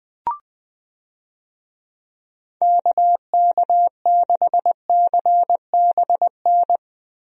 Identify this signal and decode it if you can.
Morse code, heard as a keyed tone, about 20 wpm, 700 Hz. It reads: KK6CBN
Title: USA Call Signs - 6 Characters - Repeated 2x - 20wpm